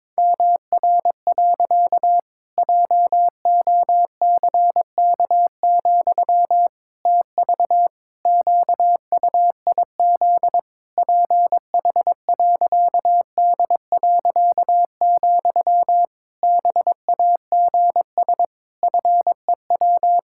Morse code, heard as a keyed tone, about 22 wpm, 700 Hz. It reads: MR. JOCK, TV QUIZ PH.D., BAGS FEW
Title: Pangram Challenge v2 - 22wpm